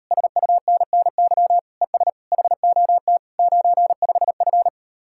Morse code, heard as a keyed tone, about 38 wpm, 700 Hz. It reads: SUNNY ES HOT 95F